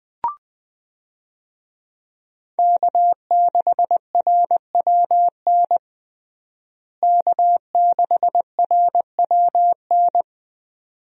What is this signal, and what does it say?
Morse code, heard as a keyed tone, about 21 wpm, 700 Hz. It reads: K6RWN K6RWN
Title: USA Call Signs - 5 Characters - Repeated 2x - 20wpm